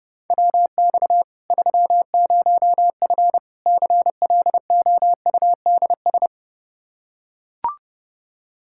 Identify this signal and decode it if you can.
Morse code, heard as a keyed tone, about 32 wpm, 700 Hz. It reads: WX 30F CLOUDS
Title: Practice QSO - Exchange 2 - Info - 30wpm